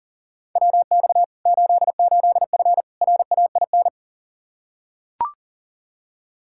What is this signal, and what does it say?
Morse code, heard as a keyed tone, about 40 wpm, 700 Hz. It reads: WX 88F RAIN
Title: Practice QSO - Exchange 2 - Info - 40wpm